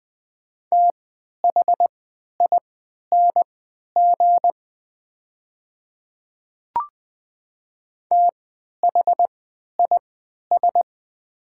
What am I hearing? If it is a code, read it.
Morse code, heard as a keyed tone, about 20 wpm, spaced as for 11 wpm, 700 Hz. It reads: THING THIS
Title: 3x Character Spacing - Top 100 Words - 20wpm